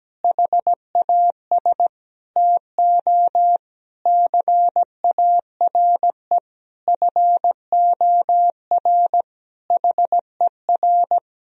Morse code, heard as a keyed tone, about 17 wpm, 700 Hz. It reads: HAS TO CARE FOR HER
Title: Sentences from Top 1000 Words - Review Part 3 - 17wpm